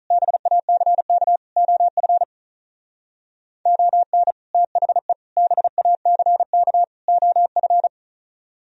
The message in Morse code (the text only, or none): BACK OF ON THE BACK OF